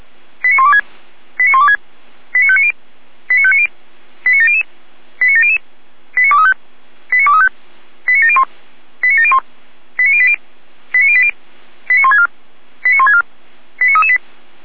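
Five-tone selective call (ZVEI), heard as quick runs of five tones, tones of about 80 ms. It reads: 89126 89126 89590 89590 89700 89700 89235 89235 89701 89701 89080 89080 89164 89164 89338
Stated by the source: ZVEI five-tone selective call 89115 received on an SDR.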